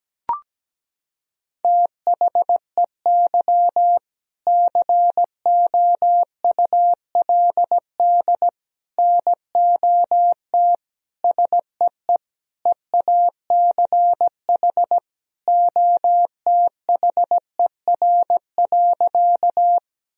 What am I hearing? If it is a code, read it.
Morse code, heard as a keyed tone, about 17 wpm, 700 Hz. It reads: THEY COULD NOT SEE EACH OTHER.
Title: Sentences from Top 1000 Words - Review Part 1 - 17wpm